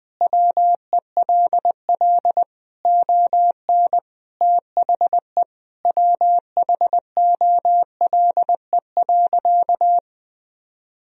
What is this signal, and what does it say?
Morse code, heard as a keyed tone, about 20 wpm, 700 Hz. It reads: WELL ON THE WHOLE.
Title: Sentences from Top 1000 Words - Review Part 1 - 20wpm